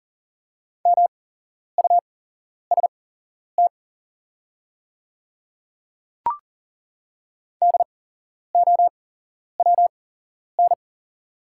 Morse code, heard as a keyed tone, about 40 wpm, spaced as for 11 wpm, 690 Hz. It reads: MUST DOWN
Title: 8x Character Spacing - Top 200 Words - 40wpm